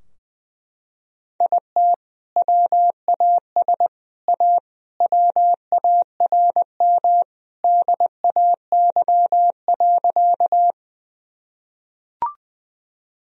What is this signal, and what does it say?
Morse code, heard as a keyed tone, about 21 wpm, 700 Hz. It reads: IT WAS A WARM DAY.
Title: Sentences from Top 1000 Words - Review Part 2 - 20wpm